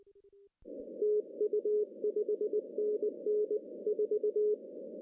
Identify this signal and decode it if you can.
Morse code, heard as a keyed tone, about 20 wpm, 420 Hz. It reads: TU 5NN 4
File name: CW / cqww-cw-k5tr